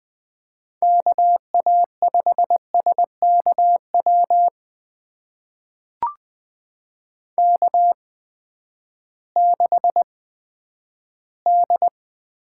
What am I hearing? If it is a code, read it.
Morse code, heard as a keyed tone, about 20 wpm, spaced as for 11 wpm, 700 Hz. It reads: KA5SKW K6D
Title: USA Call Signs - 8x Character Spacing - 20wpm